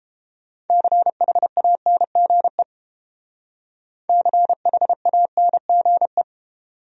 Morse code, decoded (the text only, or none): CHANGE CHANGE